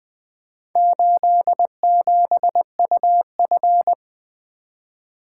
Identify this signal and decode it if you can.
Morse code, heard as a keyed tone, about 20 wpm, 700 Hz. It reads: 87UF
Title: Four Letter-Number Combinations - 20wpm